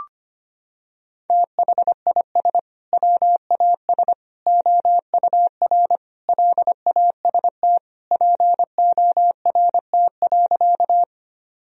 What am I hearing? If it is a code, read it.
Morse code, heard as a keyed tone, about 25 wpm, 700 Hz. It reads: THIS WAS OUR LAST PORT.
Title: Sentences from Top 1000 Words - Review Part 1 - 25wpm